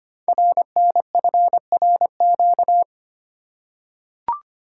Morse code, heard as a keyed tone, about 25 wpm, 700 Hz. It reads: RNFRQ
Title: Code Groups - 5 Letters - 25wpm